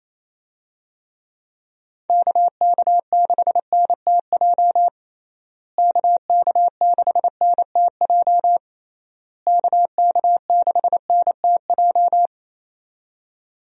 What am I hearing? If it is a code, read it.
Morse code, heard as a keyed tone, about 28 wpm, 700 Hz. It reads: KK6NTJ KK6NTJ KK6NTJ
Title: USA Call Signs - 6 Characters - Repeated 3x - 28wpm